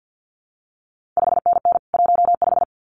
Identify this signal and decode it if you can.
Morse code, heard as a keyed tone, about 50 wpm, 700 Hz. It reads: 5NN 15
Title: CQ WW DX Contest Exchange - 50wpm